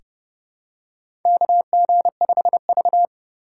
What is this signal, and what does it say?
Morse code, heard as a keyed tone, about 32 wpm, 700 Hz. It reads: KG5V
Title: K1USN SST QSO Practice - 30wpm